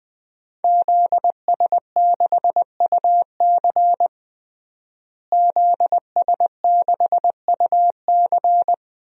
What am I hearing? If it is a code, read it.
Morse code, heard as a keyed tone, about 20 wpm, 700 Hz. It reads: ZS6UC ZS6UC
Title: DXCC-100 Call Signs - 5 Characters - Repeated 2x - 20wpm